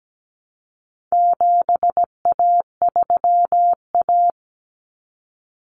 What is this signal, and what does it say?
Morse code, heard as a keyed tone, about 17 wpm, 700 Hz. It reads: 7A3A
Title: Common Cut Numbers - 4-Digit 17wpm